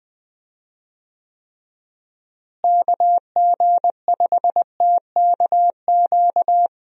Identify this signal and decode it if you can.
Morse code, heard as a keyed tone, about 21 wpm, 700 Hz. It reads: KG5TKQ
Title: USA Call Signs - 6 Characters - Repeated 2x - 20wpm